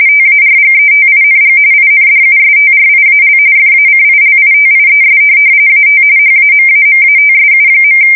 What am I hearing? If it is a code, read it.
RTTY, about 45.45 baud. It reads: TARGETING INDUSTRIAL FACILITIES AT HAMAMATSU.